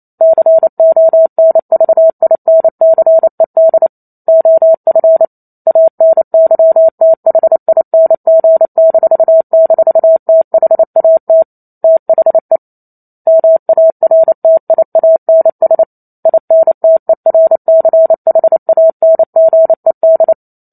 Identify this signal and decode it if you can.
Morse code, heard as a keyed tone, about 28 wpm, 650 Hz. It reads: CONVINCED OF ANYTHING--THAT THE MARTIANS INTERCHANGED
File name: Chapter 2 - 28 WPM